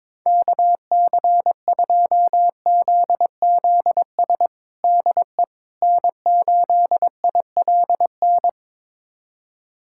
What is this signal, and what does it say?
Morse code, heard as a keyed tone, about 22 wpm, 700 Hz. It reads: KC2ZZS DE N8ILN
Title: USA Callsign DE Callsign 22wpm